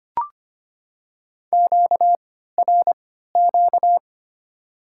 Morse code, heard as a keyed tone, about 25 wpm, 700 Hz. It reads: Q R Q
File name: Top 90 QSO Elements - 3x Character Spacing - 25wpm